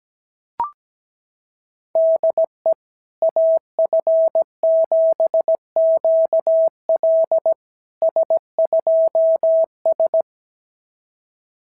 Morse code, heard as a keyed tone, about 17 wpm, 690 Hz. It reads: DE AF7QL S2S
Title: SOTA QSO Practice 17wpm